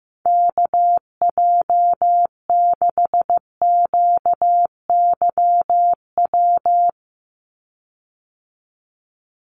Morse code, heard as a keyed tone, about 15 wpm, 700 Hz. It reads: KJ6QYW
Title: USA Call Signs 15wpm